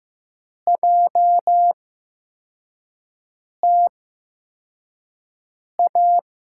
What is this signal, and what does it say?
Morse code, heard as a keyed tone, about 15 wpm, spaced as for 4 wpm, 700 Hz. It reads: JTA
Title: USA Call Signs - 5 Characters - 8x Character Spacing - 15wpm